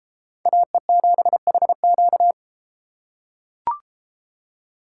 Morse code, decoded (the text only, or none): AE7HQ